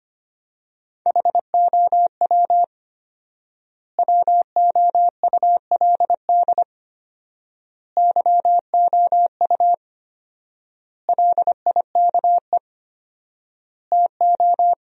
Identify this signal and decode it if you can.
Morse code, heard as a keyed tone, about 25 wpm, 700 Hz. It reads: HOW WOULD YOU LIKE TO
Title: 4x Word-Spacing - Sentences from Top 100 Words - 25wpm